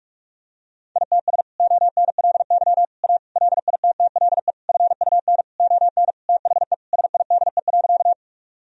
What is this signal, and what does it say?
Morse code, heard as a keyed tone, about 41 wpm, 700 Hz. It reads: ITS ONLY A LITTLE FUN ON THE SIDE.
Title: Sentences from Top 1000 Words - Review Part 4 - 45wpm